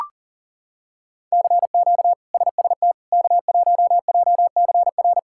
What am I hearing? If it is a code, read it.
Morse code, heard as a keyed tone, about 40 wpm, 700 Hz. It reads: CQ SST K1JCR
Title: K1USN SST QSO Practice - 40wpm